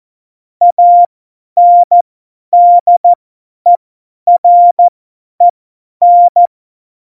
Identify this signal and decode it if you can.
Morse code, heard as a keyed tone, about 14 wpm, spaced as for 10 wpm, 700 Hz. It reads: ANDEREN
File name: Morsekurs / Witze / Witz 05. Juni 2018, 50 F70